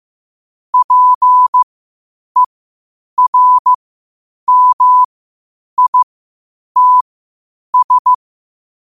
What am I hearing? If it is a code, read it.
Morse code, heard as a keyed tone, about 15 wpm, spaced as for 8 wpm, 1000 Hz. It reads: PERMITS
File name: qotdfiles / 8wpm